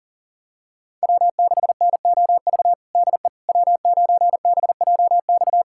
Morse code, heard as a keyed tone, about 40 wpm, 700 Hz. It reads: W6NOV DE W9BJX
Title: USA Callsign DE Callsign - 5 Characters - 40wpm